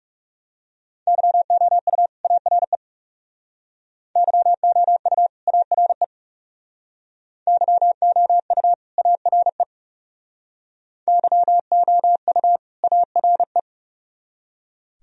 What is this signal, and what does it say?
Morse code, heard as a keyed tone, about 37 wpm, 700 Hz. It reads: YOU ARE YOU ARE YOU ARE YOU ARE
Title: Speed-Racing - Sets of 2 Words 30wpm